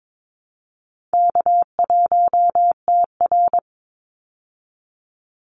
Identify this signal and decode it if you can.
Morse code, heard as a keyed tone, about 22 wpm, 700 Hz. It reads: K1TR
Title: Easy Call Signs Repeated 3x - 22wpm